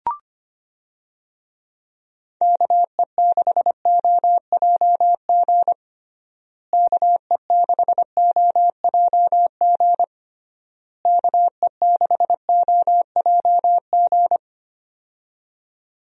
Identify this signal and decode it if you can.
Morse code, heard as a keyed tone, about 25 wpm, 700 Hz. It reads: KE6OJG KE6OJG KE6OJG
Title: USA Call Signs - 6 Characters - Repeated 3x - 25wpm